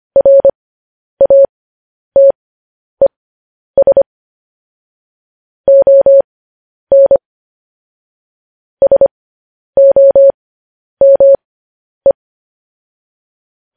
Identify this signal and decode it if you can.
Morse code, heard as a keyed tone, about 25 wpm, spaced as for 10 wpm, 550 Hz. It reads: RATES ON SOME
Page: news headlines in morse code at 10 wpm